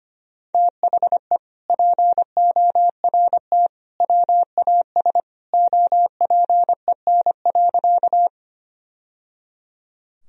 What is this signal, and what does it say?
Morse code, heard as a keyed tone, about 25 wpm, 700 Hz. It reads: THE PORT WAS OPEN.
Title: Sentences from Top 1000 Words - Review Part 1 - 25wpm